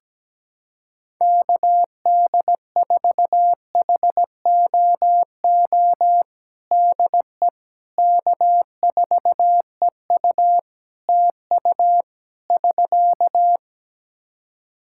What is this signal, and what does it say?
Morse code, heard as a keyed tone, about 17 wpm, 700 Hz. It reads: KD4HOO DE K4EU TU <SK>
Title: Practice QSO - Exchange 4 - Ending - 17wpm